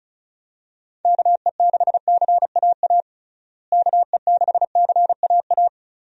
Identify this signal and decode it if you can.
Morse code, heard as a keyed tone, about 36 wpm, 700 Hz. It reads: KE6CAA KE6CAA
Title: USA Call Signs - 6 Characters - Repeated 3x - 35wpm